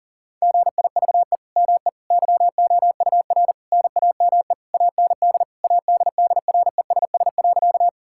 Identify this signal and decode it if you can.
Morse code, heard as a keyed tone, about 40 wpm, 700 Hz. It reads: GIVE ME YOUR NAME AND ADDRESS.
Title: Sentences from Top 1000 Words - Review Part 6 - 40wpm